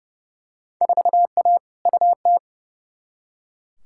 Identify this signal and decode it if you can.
Morse code, heard as a keyed tone, about 31 wpm, 700 Hz. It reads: 4A UT
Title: Field Day Exchanges - 30wpm